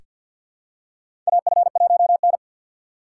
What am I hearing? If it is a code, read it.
Morse code, heard as a keyed tone, about 47 wpm, 700 Hz. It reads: AF1N
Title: USA Call Signs - 3x Character Spacing - 50wpm